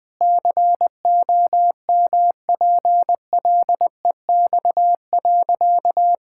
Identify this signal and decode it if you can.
Morse code, heard as a keyed tone, about 21 wpm, 700 Hz. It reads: COMPLEX.
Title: Sentences with 4-Word Phrases - Spoken Beginning - 20wpm